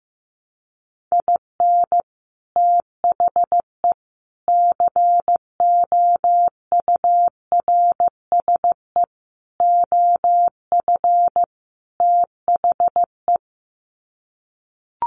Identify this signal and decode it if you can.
Morse code, heard as a keyed tone, about 15 wpm, 700 Hz. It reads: IN THE COURSE OF THE
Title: Sets of 5 Words 15wpm